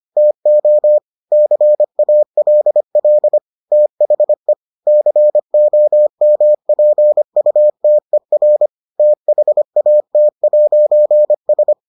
Morse code, heard as a keyed tone, about 25 wpm, 600 Hz. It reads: TO CALL THE COMPUTER THAT'S